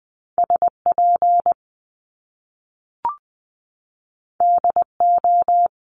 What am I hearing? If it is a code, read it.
Morse code, heard as a keyed tone, about 20 wpm, 700 Hz. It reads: SP DO
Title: Top Two Letter Combinations in Words 20wpm